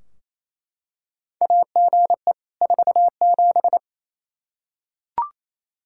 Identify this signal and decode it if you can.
Morse code, heard as a keyed tone, about 27 wpm, 700 Hz. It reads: AGE 47
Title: Practice QSO - Exchange 3 - More Info - 28wpm